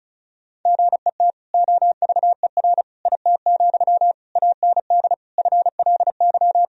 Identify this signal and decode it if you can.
Morse code, heard as a keyed tone, about 36 wpm, 700 Hz. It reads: GET OVER IT, AND FLY
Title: Sentences from Top 1000 Words - Review Part 2 - 35wpm